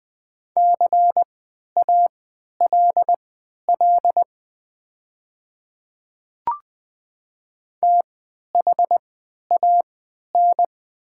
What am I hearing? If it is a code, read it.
Morse code, heard as a keyed tone, about 20 wpm, spaced as for 11 wpm, 700 Hz. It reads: CALL THAN
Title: 3x Character Spacing - Top 100 Words - 20wpm